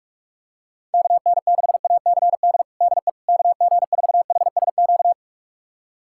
Morse code, heard as a keyed tone, about 46 wpm, 700 Hz. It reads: KN6ACD DE KG4HSQ
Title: USA Callsign DE Callsign 45wpm